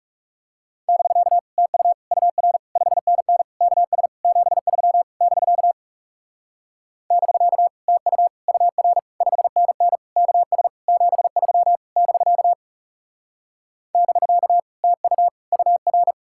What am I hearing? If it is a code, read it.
Morse code, heard as a keyed tone, about 39 wpm, 700 Hz. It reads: <BK> TU UR 5NN KS 73 <BK> <BK> TU UR 5NN KS 73 <BK> <BK> TU UR